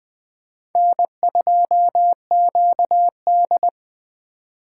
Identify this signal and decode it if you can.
Morse code, heard as a keyed tone, about 20 wpm, 700 Hz. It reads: N2QD
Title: USA Call Signs - 4 Characters - 2x Character Spacing - 20wpm